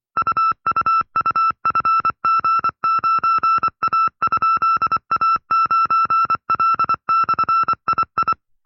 Morse code, heard as a keyed tone, about 24 wpm, 1330 Hz. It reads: UUUFG9A?A9L/II
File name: FreeFiles / CW-practice / 20WPM-side2